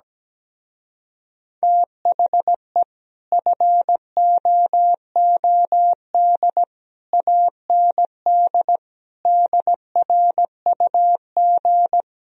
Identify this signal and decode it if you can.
Morse code, heard as a keyed tone, about 17 wpm, 700 Hz. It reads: THE FOOD AND DRUG
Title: Speed-Racing - Sets of 5 Words 17wpm